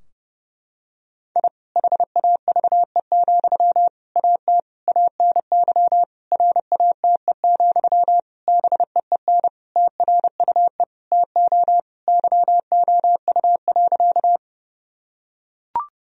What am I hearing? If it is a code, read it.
Morse code, heard as a keyed tone, about 30 wpm, 700 Hz. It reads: I HAVE, AT ANY RATE, BEEN TRUE TO YOU.
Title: Sentences from Top 1000 Words - Review Part 4 - 30wpm